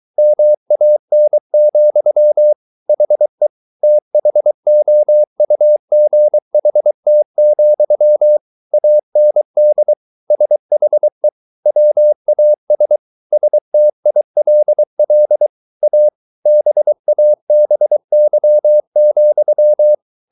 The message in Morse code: MAN, HE THOUGHT, AND SHE WAS STILL A BABY,

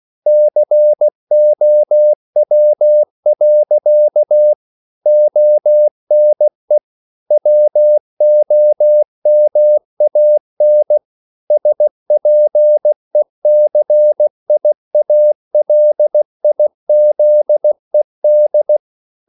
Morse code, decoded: COW. ONE WOMAN SPECIALIZED

